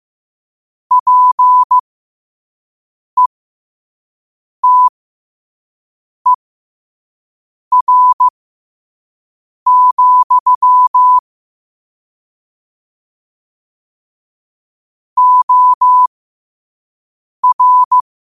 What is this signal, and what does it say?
Morse code, heard as a keyed tone, about 15 wpm, spaced as for 5 wpm, 1000 Hz. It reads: PETER, OR
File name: qotdfiles / 5wpm